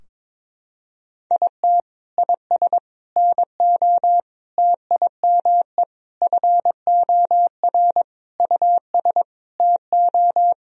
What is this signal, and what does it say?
Morse code, heard as a keyed tone, about 22 wpm, 700 Hz. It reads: IT IS NO TIME FOR US TO
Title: Sentences from Top 1000 Words - Review Part 3 - 22wpm